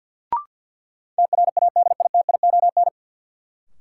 Morse code, heard as a keyed tone, about 50 wpm, 700 Hz. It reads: TRADITION
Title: Sentences from Top 1500 Words - New Word Drill Alt - 50wpm